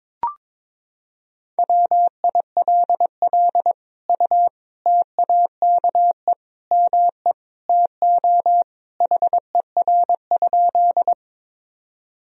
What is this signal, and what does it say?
Morse code, heard as a keyed tone, about 22 wpm, 700 Hz. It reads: WILL U TAKE ME TO HER?
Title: Sentences from Top 200 Words and Abbreviations 22wpm